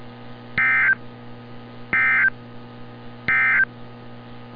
Emergency Alert System (SAME) tones, heard as harsh buzzing bursts. It is heard from 0.5 to 3.7 s.
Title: NOAA Weather Radio - Altoona, PA